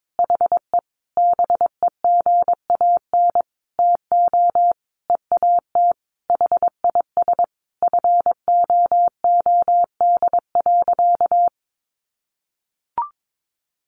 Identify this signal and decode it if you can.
Morse code, heard as a keyed tone, about 22 wpm, 700 Hz. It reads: HE BEGAN TO EAT HIS FOOD.
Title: Sentences from Top 1100 Words 22wpm